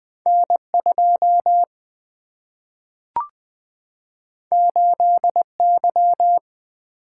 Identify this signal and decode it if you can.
Morse code, heard as a keyed tone, about 20 wpm, 700 Hz. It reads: N2 8Y
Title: Single-Letter Single-Number 20wpm